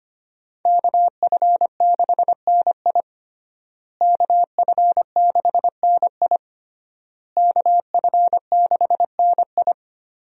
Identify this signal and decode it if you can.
Morse code, heard as a keyed tone, about 25 wpm, 700 Hz. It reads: KF6NI KF6NI KF6NI
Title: USA Call Signs - 5 Characters - Repeated 3x - 25wpm